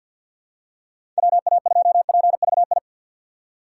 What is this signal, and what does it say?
Morse code, heard as a keyed tone, about 53 wpm, 700 Hz. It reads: WA2PVI
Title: USA Call Signs - 6x Character Spacing - 50wpm